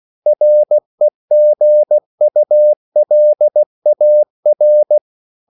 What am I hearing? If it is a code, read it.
Morse code, heard as a keyed tone, about 16 wpm, 600 Hz. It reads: REGULAR